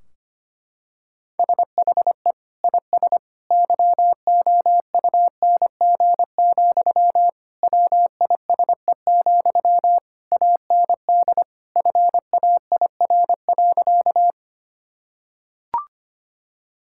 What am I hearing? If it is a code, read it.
Morse code, heard as a keyed tone, about 25 wpm, 700 Hz. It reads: SHE IS YOUNG, WISE, AND FAIR.
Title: Sentences from Top 1600 Words 25wpm